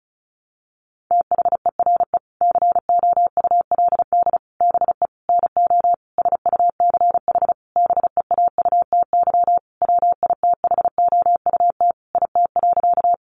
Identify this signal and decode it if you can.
Morse code, heard as a keyed tone, about 34 wpm, 700 Hz. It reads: THERE COULD BE NO SUCH BEAUTY WITHOUT IT.
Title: Sentences from Top 600 Words 35wpm